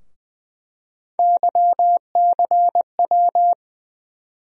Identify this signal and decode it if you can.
Morse code, heard as a keyed tone, about 20 wpm, 700 Hz. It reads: YCW
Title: USA Call Sign Postfixes - 3 Letters 20wpm